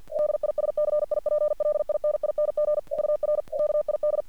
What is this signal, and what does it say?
Morse code, heard as a keyed tone, about 48 wpm, 600 Hz. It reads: DISCIPLINING UR KID
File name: archived / CRSnet